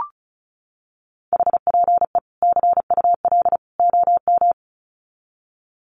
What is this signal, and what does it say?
Morse code, heard as a keyed tone, about 34 wpm, 700 Hz. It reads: HPE CUL OM
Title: Common QSO Phrases 35wpm